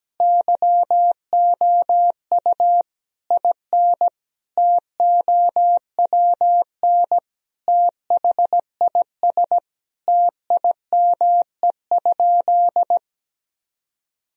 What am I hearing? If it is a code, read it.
Morse code, heard as a keyed tone, about 17 wpm, 700 Hz. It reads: YOU IN TOWN THIS TIME?
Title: Sentences from Top 1000 Words - Review Part 2 - 17wpm